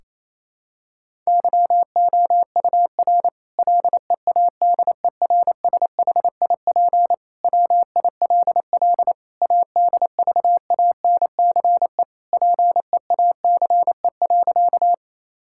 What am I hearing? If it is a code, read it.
Morse code, heard as a keyed tone, about 28 wpm, 700 Hz. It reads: YOUR LEADERSHIP WILL ADVANCE PEACE.
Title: Sentences from Top 1600 Words 28wpm